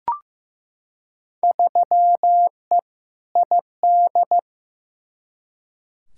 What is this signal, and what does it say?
Morse code, heard as a keyed tone, about 15 wpm, 700 Hz. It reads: 3E ID
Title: Field Day Exchanges Encore - 15wpm